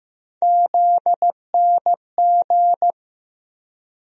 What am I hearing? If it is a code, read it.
Morse code, heard as a keyed tone, about 15 wpm, 700 Hz. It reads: ZNG
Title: USA Call Sign Postfixes - 3 Letters 15wpm